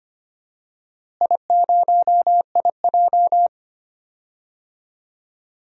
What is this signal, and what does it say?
Morse code, heard as a keyed tone, about 25 wpm, 700 Hz. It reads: I0IJ
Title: DXCC-100 Call Signs - 4 Characters - Repeated 2x - 25wpm